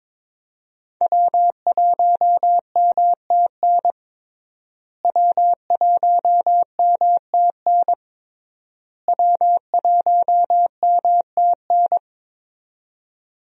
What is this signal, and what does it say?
Morse code, heard as a keyed tone, about 22 wpm, 700 Hz. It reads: W1MTN W1MTN W1MTN
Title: USA Call Signs - 5 Characters - Repeated 3x - 22wpm